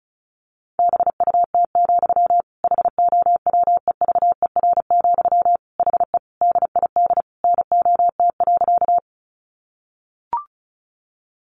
Morse code, heard as a keyed tone, about 34 wpm, 700 Hz. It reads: BUT, HOWEVER, HE DID NOT.
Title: Sentences from Top 600 Words 35wpm